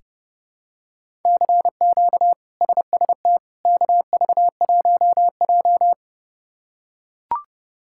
Morse code, heard as a keyed tone, about 30 wpm, 700 Hz. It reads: CQ SST KV1J
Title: K1USN SST QSO Practice - 30wpm